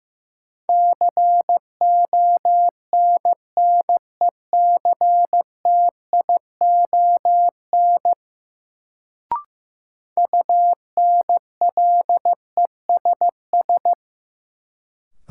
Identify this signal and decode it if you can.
Morse code, heard as a keyed tone, about 15 wpm, 700 Hz. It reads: CONNECTION UNLESS
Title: Sentences from Top 1300 Words - New Word Drill Alt - 15wpm